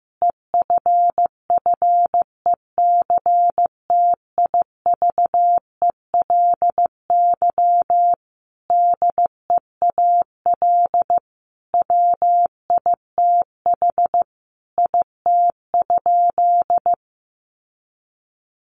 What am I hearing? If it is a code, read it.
Morse code, heard as a keyed tone, about 15 wpm, 700 Hz. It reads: EFFECTIVELY DEAL WITH IT?
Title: Sentences from Top 1400 Words 15wpm